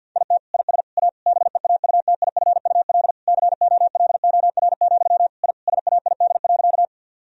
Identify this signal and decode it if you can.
Morse code, heard as a keyed tone, about 49 wpm, 700 Hz. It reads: IT IS A BEAUTIFUL COLOR, I SAID.
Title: Sentences from Top 1000 Words - Review Part 2 - 50wpm